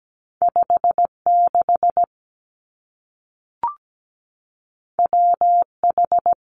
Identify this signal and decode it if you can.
Morse code, heard as a keyed tone, about 17 wpm, 700 Hz. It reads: 56 WH
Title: Two Challenging Letter-Number Combinations 17wpm